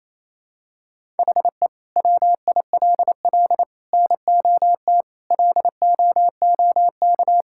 Morse code, heard as a keyed tone, about 28 wpm, 700 Hz. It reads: HE WILL NOT LOOK
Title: Sentences from Top 1000 Words - Review Part 5 - 28wpm